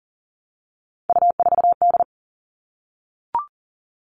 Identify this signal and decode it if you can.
Morse code, heard as a keyed tone, about 40 wpm, 700 Hz. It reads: U4D
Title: All Cut Numbers - 3-Digit 40wpm